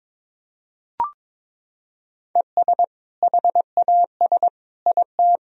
Morse code, heard as a keyed tone, about 22 wpm, 700 Hz. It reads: ES HAS IT
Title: Sentences from Top 100 Words and Abbreviations 22wpm